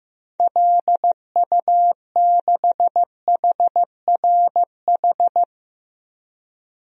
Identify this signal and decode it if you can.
Morse code, heard as a keyed tone, about 15 wpm, 700 Hz. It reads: LU6HRH
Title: DXCC-100 Call Signs - 6 Characters - 15wpm